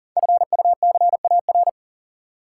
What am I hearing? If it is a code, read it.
Morse code, heard as a keyed tone, about 40 wpm, 690 Hz. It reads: FUCAR
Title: Code Groups - 5 Letter:Numbers - 40wpm